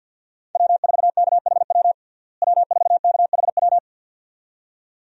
Morse code, heard as a keyed tone, about 51 wpm, 700 Hz. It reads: W4KHW W4KHW
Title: USA Call Signs - 5 Characters - Repeated 2x - 50wpm